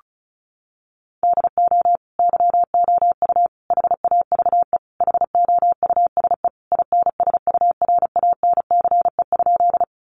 Morse code, heard as a keyed tone, about 34 wpm, 700 Hz. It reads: DO YOU HAVE HOUSE INSURANCE?
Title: Sentences from Top 1100 Words 35wpm